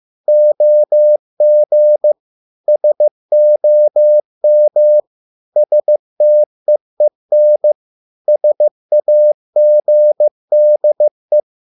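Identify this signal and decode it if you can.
Morse code, heard as a keyed tone, about 15 wpm, 600 Hz. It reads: OG SOM STEEN SAGDE
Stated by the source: Morse code at a slow 15 wpm from CW med Gnister.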